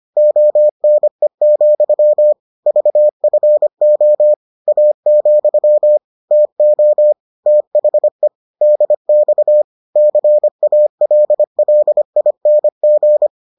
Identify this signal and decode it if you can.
Morse code, heard as a keyed tone, about 25 wpm, 600 Hz. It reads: ONE, VFO A, TO THE DX CALLING